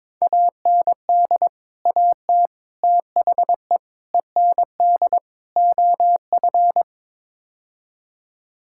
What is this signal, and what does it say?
Morse code, heard as a keyed tone, about 21 wpm, 700 Hz. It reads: AND AT THE END OF
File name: practice / 22wpm / Build A Phrase - 5-Word Phrases - 22wpm